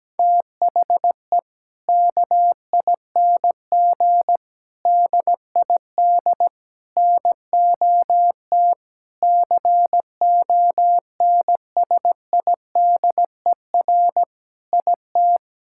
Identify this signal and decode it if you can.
Morse code, heard as a keyed tone, about 17 wpm, 700 Hz. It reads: THE KING DID NOT CONSIDER IT
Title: Sentences from Top 1000 Words - Review Part 5 - 17wpm